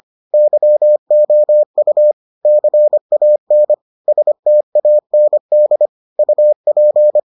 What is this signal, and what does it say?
Morse code, heard as a keyed tone, about 25 wpm, 600 Hz. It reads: YOU CAN STAND UP